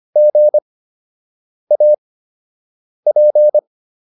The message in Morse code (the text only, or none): GAP